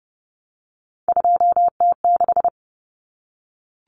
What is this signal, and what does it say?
Morse code, heard as a keyed tone, about 30 wpm, 700 Hz. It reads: 2T6
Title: Common Cut Numbers - 3-Digit 30wpm